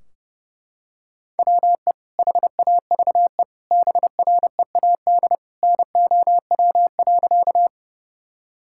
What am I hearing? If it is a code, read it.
Morse code, heard as a keyed tone, about 32 wpm, 700 Hz. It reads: WE HAVE BREAD NOW.
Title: Sentences from Top 1600 Words 30wpm